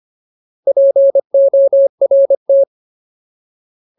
Morse code, heard as a keyed tone, about 25 wpm, 550 Hz. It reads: PORT
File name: Ultimate Word List / 25WPM Ultimate